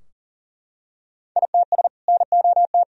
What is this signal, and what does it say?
Morse code, heard as a keyed tone, about 40 wpm, 700 Hz. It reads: ITS NOT
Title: Sentences from Top 1000 Words - Review Part 3 - 40wpm